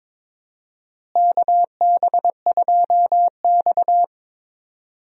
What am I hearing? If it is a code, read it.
Morse code, heard as a keyed tone, about 22 wpm, 700 Hz. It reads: KB2X
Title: USA Call Signs - 4 Characters - Repeated 3x - 22wpm